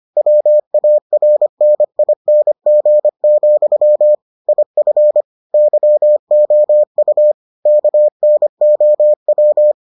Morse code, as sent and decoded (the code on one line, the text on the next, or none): .-- .- .-. -. .. -. --. --..-- / .. ..-. / -.-- --- ..- / -.- -. --- .--
WARNING, IF YOU KNOW